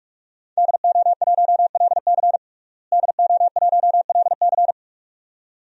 Morse code, heard as a keyed tone, about 42 wpm, 700 Hz. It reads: DO1LC DO1LC
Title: DXCC-100 Call Signs - 5 Characters - Repeated 2x - 45wpm